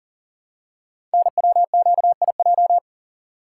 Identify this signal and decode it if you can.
Morse code, heard as a keyed tone, about 41 wpm, 690 Hz. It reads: NWQIJ
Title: Code Groups - 5 Letter:Numbers - 40wpm